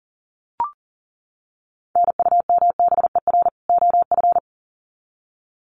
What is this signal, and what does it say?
Morse code, heard as a keyed tone, about 40 wpm, 700 Hz. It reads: NUMBER OF
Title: Sets of 2 Words 40wpm